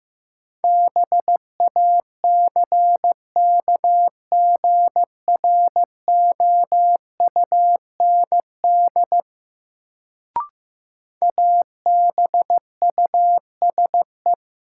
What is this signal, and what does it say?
Morse code, heard as a keyed tone, about 15 wpm, 700 Hz. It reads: BACKGROUND ABUSE
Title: Sentences from Top 1300 Words - New Word Drill - 15wpm